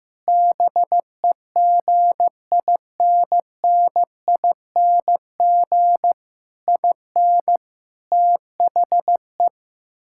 Morse code, heard as a keyed tone, about 15 wpm, 700 Hz. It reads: BEGINNING IN THE
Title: Sentences from Top 1000 Words - Review Part 6 - 15wpm